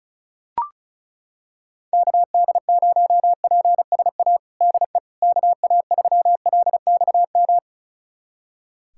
Morse code, heard as a keyed tone, about 34 wpm, 700 Hz. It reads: KD0PSA DE KA3LXM
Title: USA Callsign DE Callsign - 6 Characters - 35wpm